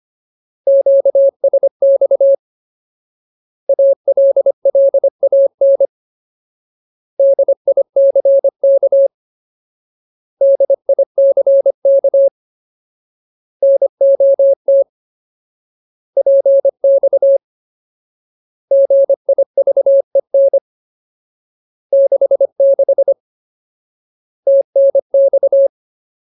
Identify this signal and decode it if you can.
Morse code, heard as a keyed tone, about 25 wpm, 550 Hz. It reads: QSX ALLAN DICK DICK NOT PX GIVEN 66 TNX
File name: Ultimate Word List / 25WPM Ultimate